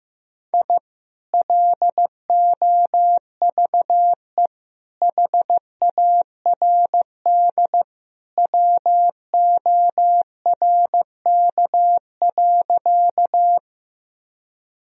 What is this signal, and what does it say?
Morse code, heard as a keyed tone, about 15 wpm, 700 Hz. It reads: I LOVE HARD WORK.
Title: Sentences from Top 1000 Words - Review Part 1 - 15wpm